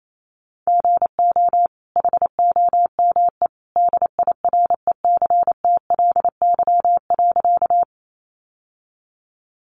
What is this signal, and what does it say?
Morse code, heard as a keyed tone, about 28 wpm, 700 Hz. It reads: GO HOME DIRECTLY.
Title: Sentences from Top 800 Words 28wpm